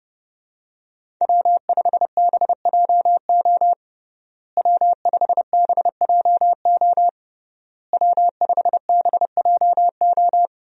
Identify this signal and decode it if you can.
Morse code, heard as a keyed tone, about 32 wpm, 700 Hz. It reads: W5BJO W5BJO W5BJO
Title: USA Call Signs - 5 Characters - Repeated 3x - 30wpm